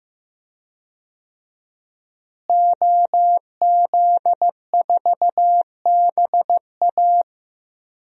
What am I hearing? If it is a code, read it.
Morse code, heard as a keyed tone, about 15 wpm, 700 Hz. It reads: OZ4BA
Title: DXCC-100 Call Signs - 5 Characters - Repeated 3x - 15wpm